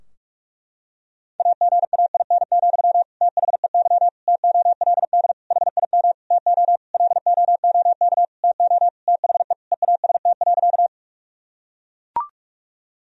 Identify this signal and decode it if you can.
Morse code, heard as a keyed tone, about 41 wpm, 700 Hz. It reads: AGAIN, THEY TOLD HIM TO LOOK TO THE EAST.
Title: Sentences from Top 1000 Words - Review Part 3 - 45wpm